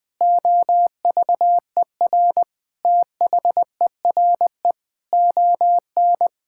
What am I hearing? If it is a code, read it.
Morse code, heard as a keyed tone, about 20 wpm, 700 Hz. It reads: OVER THERE ON
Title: Sentences from Top 1000 Words - Review Part 5 - 20wpm